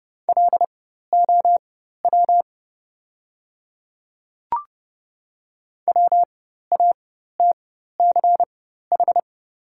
Morse code, heard as a keyed tone, about 30 wpm, spaced as for 14 wpm, 700 Hz. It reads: LOW WATCH
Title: 4x Character Spacing - Top 300 Words - 30wpm